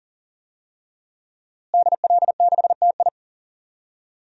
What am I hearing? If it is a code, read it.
Morse code, heard as a keyed tone, about 40 wpm, 700 Hz. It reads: DL6TI
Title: DXCC-100 Call Signs - 5 Characters - Repeated 2x - 40wpm